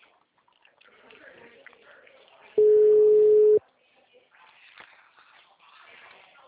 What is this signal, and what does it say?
Ringback tone (Europe, 425 Hz), with speech.